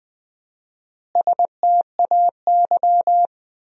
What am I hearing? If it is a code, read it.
Morse code, heard as a keyed tone, about 20 wpm, 700 Hz. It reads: STAY
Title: 4x Word-Spacing - Sentences from Top 500 Words - 20wpm